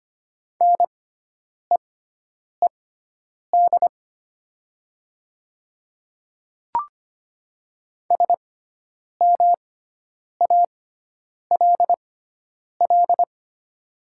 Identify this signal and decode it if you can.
Morse code, heard as a keyed tone, about 25 wpm, spaced as for 9 wpm, 700 Hz. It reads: NEED SMALL
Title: 6x Character Spacing - Top 200 Words - 25wpm